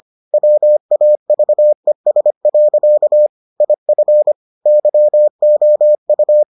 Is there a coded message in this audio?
Yes: Morse code from 0.3 to 6.4 s, heard as a keyed tone, about 25 wpm, 600 Hz. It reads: WAVES. IF YOU